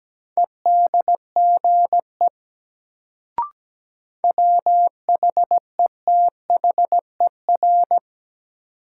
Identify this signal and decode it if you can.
Morse code, heard as a keyed tone, about 17 wpm, 700 Hz. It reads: EDGE WHETHER
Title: Sets of 3 Words - New Word Drill - 17wpm